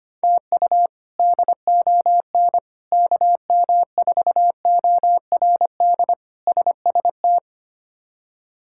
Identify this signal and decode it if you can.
Morse code, heard as a keyed tone, about 25 wpm, 700 Hz. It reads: TU DON KM4ORD SST
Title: K1USN SST QSO Practice - 25wpm